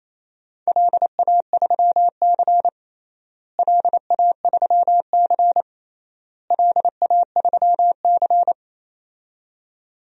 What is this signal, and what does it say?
Morse code, heard as a keyed tone, about 28 wpm, 700 Hz. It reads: LA3C LA3C LA3C
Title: DXCC-100 Call Signs - 4 Characters - Repeated 3x - 28wpm